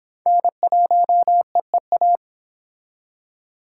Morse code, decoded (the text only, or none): N1EEA